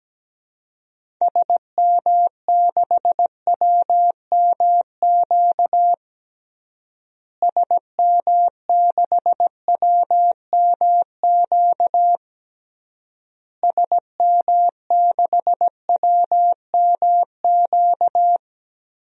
Morse code, heard as a keyed tone, about 17 wpm, 700 Hz. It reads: SM6WMQ SM6WMQ SM6WMQ
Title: DXCC-100 Call Signs - 6 Characters - Repeated 3x - 17wpm